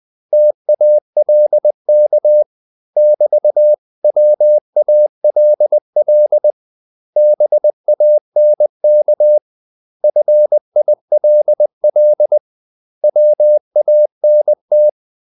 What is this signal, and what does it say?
Morse code, heard as a keyed tone, about 20 wpm, 600 Hz. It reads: TALK <BT>WALL BANK FILL WANT
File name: CW / Practice / 500MostCommonEnglishWords20